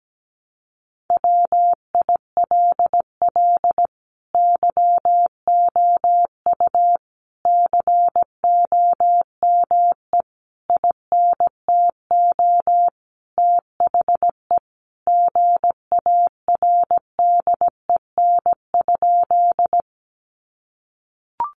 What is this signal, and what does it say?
Morse code, heard as a keyed tone, about 17 wpm, 700 Hz. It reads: WILL YOU COME INTO THE GARDEN?
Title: Sentences from Top 700 Words 17wpm